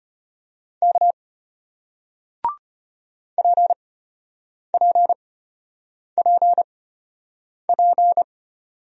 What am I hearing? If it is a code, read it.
Morse code, heard as a keyed tone, about 34 wpm, spaced as for 8 wpm, 700 Hz. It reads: K PPPP